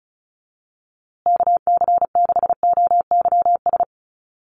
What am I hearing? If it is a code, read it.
Morse code, heard as a keyed tone, about 34 wpm, 700 Hz. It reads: KC6OYS
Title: USA Call Signs Repeated 2x 35wpm